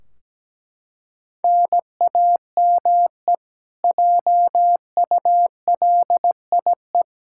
Morse code, heard as a keyed tone, about 17 wpm, 700 Hz. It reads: NAME JULIE
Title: Practice QSO - Exchange 1 - Beginning - 17wpm